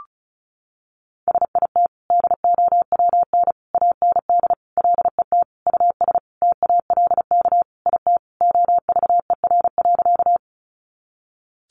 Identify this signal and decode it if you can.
Morse code, heard as a keyed tone, about 34 wpm, 700 Hz. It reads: SIT DOWN AND LET US TALK IT OVER.
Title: Sentences from Top 500 Words 35wpm